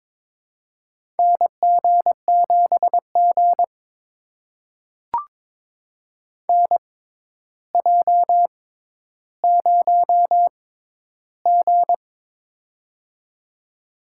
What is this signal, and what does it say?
Morse code, heard as a keyed tone, about 22 wpm, spaced as for 11 wpm, 700 Hz. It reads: NG7G NJ0G
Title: USA Call Signs - 4 Characters - 6x Character Spacing - 22wpm